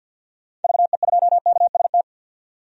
Morse code, heard as a keyed tone, about 52 wpm, 700 Hz. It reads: VE2KST
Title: DXCC-100 Call Signs - 6 Characters - 50wpm